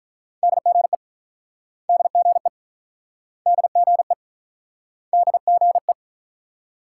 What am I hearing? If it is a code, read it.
Morse code, heard as a keyed tone, about 47 wpm, 690 Hz. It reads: DGE DGE DGE DGE